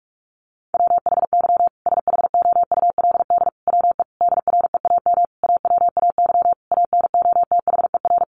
Morse code, heard as a keyed tone, about 44 wpm, 700 Hz. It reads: WHY SHOULD WE DREAM AWAY ANOTHER
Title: Sentences from Top 1500 Words 45wpm